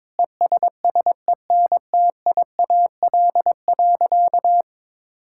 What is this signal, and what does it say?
Morse code, heard as a keyed tone, about 22 wpm, 700 Hz. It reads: ESSENTIAL.
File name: Sentences with 7-Word Phrases - 22wpm